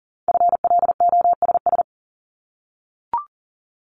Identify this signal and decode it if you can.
Morse code, heard as a keyed tone, about 40 wpm, 690 Hz. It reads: FLOSS
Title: Multi-letter Phonemes 40wpm